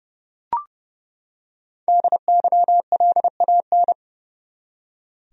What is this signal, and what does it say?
Morse code, heard as a keyed tone, about 32 wpm, 700 Hz. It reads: DYLAN
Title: USA Common Names - 30wpm